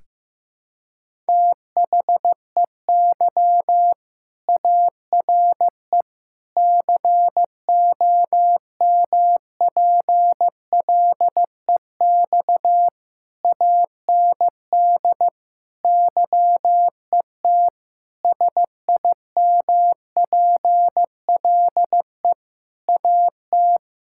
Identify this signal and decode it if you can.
Morse code, heard as a keyed tone, about 15 wpm, 700 Hz. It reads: THEY ARE COMPLEX AND YET SIMPLE AT
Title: Sentences from Top 1000 Words - Review Part 6 - 15wpm